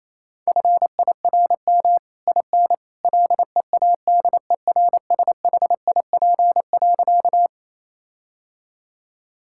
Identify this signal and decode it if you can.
Morse code, heard as a keyed tone, about 28 wpm, 700 Hz. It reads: FIRM IN LEADERSHIP.
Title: Sentences from Top 1600 Words 28wpm